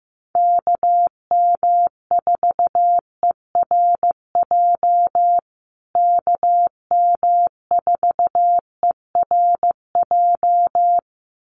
Morse code, heard as a keyed tone, about 15 wpm, 700 Hz. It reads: KM4ERJ KM4ERJ
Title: USA Call Signs Repeated 2x 15wpm